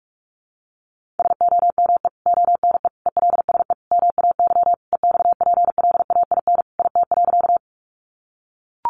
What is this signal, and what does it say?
Morse code, heard as a keyed tone, about 44 wpm, 700 Hz. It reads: SOME ONE ELSE MAY EXPLAIN IT.
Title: Sentences from Top 1100 Words 45wpm